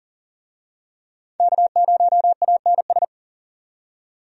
Morse code, heard as a keyed tone, about 40 wpm, 700 Hz. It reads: K0ANS
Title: USA Call Signs - 5 Characters - 40wpm